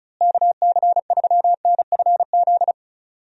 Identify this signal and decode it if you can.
Morse code, heard as a keyed tone, about 36 wpm, 700 Hz. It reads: KC3NFZ
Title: USA Call Signs - 6 Characters - Repeated 3x - 35wpm